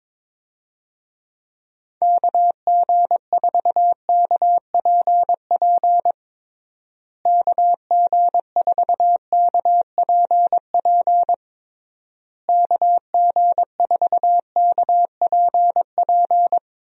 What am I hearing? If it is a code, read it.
Morse code, heard as a keyed tone, about 22 wpm, 700 Hz. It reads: KG4KPP KG4KPP KG4KPP
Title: USA Call Signs - 6 Characters - Repeated 3x - 22wpm